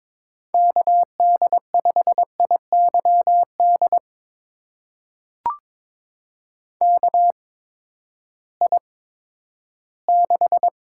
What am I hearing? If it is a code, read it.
Morse code, heard as a keyed tone, about 22 wpm, spaced as for 13 wpm, 700 Hz. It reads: KD5IYD KI6
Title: USA Call Signs - 6 Characters - 8x Character Spacing - 22wpm